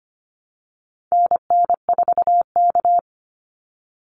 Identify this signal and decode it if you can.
Morse code, heard as a keyed tone, about 25 wpm, 700 Hz. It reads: NN4K
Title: Call Signs - 25wpm